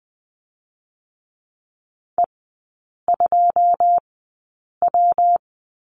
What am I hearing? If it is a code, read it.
Morse code, heard as a keyed tone, about 20 wpm, spaced as for 8 wpm, 700 Hz. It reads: E2W
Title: 121 Lesson 20e - Number 3 - Three Characters 20wpm